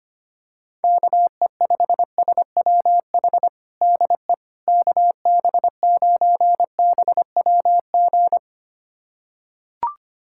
Morse code, heard as a keyed tone, about 25 wpm, 700 Hz. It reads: KE5SWH DE KB9BWG
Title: USA Callsign DE Callsign - 6 Characters - 25wpm